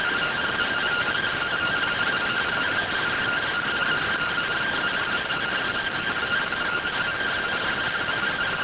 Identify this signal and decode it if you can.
Two-tone teleprinter signal, about 170 Hz shift.
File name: findings / SAGITAL fr15 nb203 3mm.gen-09 rx